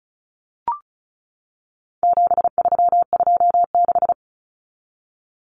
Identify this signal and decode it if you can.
Morse code, heard as a keyed tone, about 34 wpm, 700 Hz. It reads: 7326
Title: Common Cut Numbers - 4-Digit 35wpm